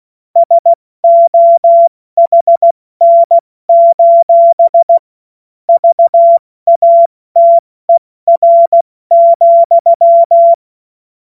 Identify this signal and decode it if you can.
Morse code, heard as a keyed tone, about 16 wpm, 680 Hz. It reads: SOHN: VATER,
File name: Witz-08-Juli-2025-70-F80